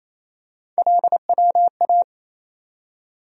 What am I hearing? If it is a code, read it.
Morse code, heard as a keyed tone, about 28 wpm, 700 Hz. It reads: LWA
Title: USA Call Sign Postfixes - 3 Letters 28wpm